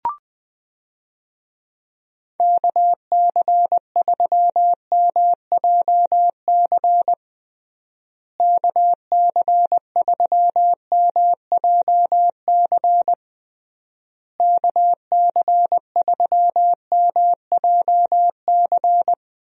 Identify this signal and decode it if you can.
Morse code, heard as a keyed tone, about 21 wpm, 700 Hz. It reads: KC3MJC KC3MJC KC3MJC
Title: USA Call Signs - 6 Characters - Repeated 3x - 20wpm